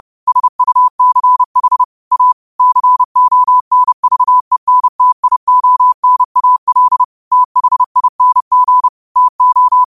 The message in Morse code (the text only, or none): SUCH A CONVENTIONAL THING TO